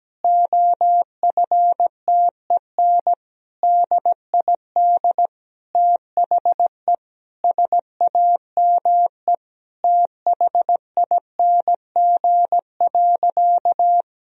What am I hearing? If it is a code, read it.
Morse code, heard as a keyed tone, about 17 wpm, 700 Hz. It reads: OFTEN DID THE SAME THING.
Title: Sentences from Top 1000 Words - Review Part 1 - 17wpm